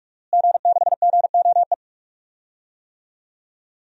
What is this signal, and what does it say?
Morse code, heard as a keyed tone, about 46 wpm, 700 Hz. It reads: G6GOE
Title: DXCC-100 Call Signs - 5 Characters - Repeated 2x - 45wpm